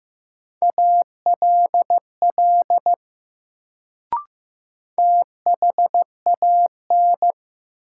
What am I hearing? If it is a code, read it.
Morse code, heard as a keyed tone, about 15 wpm, 700 Hz. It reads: ALL THAN
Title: Sentences from Top 100 Words - New Word Drill - 15wpm